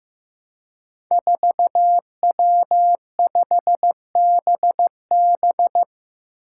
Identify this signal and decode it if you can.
Morse code, heard as a keyed tone, about 15 wpm, 700 Hz. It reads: 4W5BB
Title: Five Challenging Letter-Number Combinations 15wpm